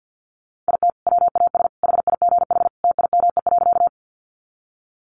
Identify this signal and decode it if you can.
Morse code, heard as a keyed tone, about 50 wpm, 700 Hz. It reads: IT WAS HIGH TIME.
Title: Sentences from Top 200 Words 50wpm